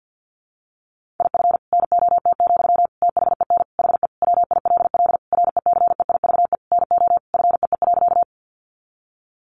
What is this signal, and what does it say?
Morse code, heard as a keyed tone, about 50 wpm, 700 Hz. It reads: IF NOT, THEN HE WILL RECEIVE NO FEE.
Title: Sentences from Top 1500 Words 50wpm